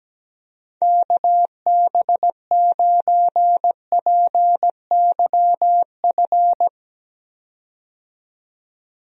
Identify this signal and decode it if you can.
Morse code, heard as a keyed tone, about 17 wpm, 700 Hz. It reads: KB9PYF
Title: USA Call Signs - 6 Characters - 17wpm